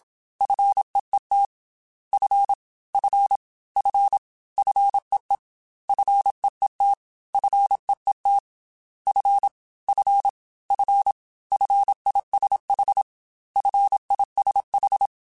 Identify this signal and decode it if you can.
Morse code, heard as a keyed tone, about 27 wpm, 800 Hz. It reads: FEET F F F FEE FEET FEET F F F FISH FISH
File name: F and L / F and L mp3 / F and L Prefixes 3x2 Twice @25